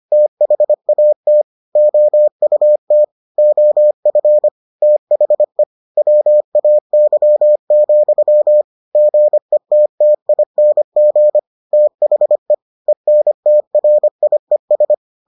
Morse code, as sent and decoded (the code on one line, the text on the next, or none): - .... .- - / --- ..- - / --- ..-. / - .... / .-- .- -.-- --..-- / --. - - .. -. --. / - .... / . -. - .-. .. ...
THAT OUT OF THE WAY, GETTING THE ENTRIES